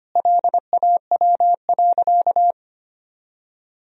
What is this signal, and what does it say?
Morse code, heard as a keyed tone, about 25 wpm, 700 Hz. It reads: LAW.
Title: Sentences from Top 1000 Words - Review Part 4 - 25wpm